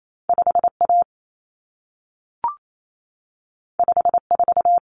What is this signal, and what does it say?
Morse code, heard as a keyed tone, about 28 wpm, 700 Hz. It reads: 5A 54
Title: Common Cut Numbers - 2-Digit 28wpm